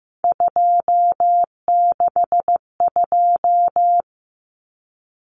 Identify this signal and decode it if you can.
Morse code, heard as a keyed tone, about 15 wpm, 700 Hz. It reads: 262
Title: Numbers - 3-Digit 15wpm